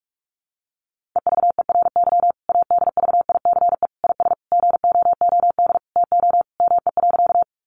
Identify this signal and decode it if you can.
Morse code, heard as a keyed tone, about 44 wpm, 700 Hz. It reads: EVERY ADVICE IS GOOD TO ME.
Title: Sentences from Top 1100 Words 45wpm